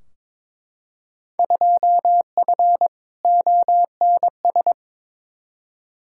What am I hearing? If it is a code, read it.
Morse code, heard as a keyed tone, about 22 wpm, 700 Hz. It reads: 2F ONS
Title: Field Day Exchanges - 22wpm